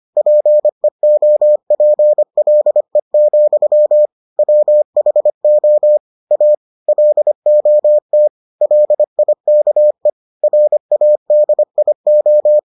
Morse code, heard as a keyed tone, about 25 wpm, 600 Hz. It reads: PEOPLE, WHO A LOT LIKE RADIO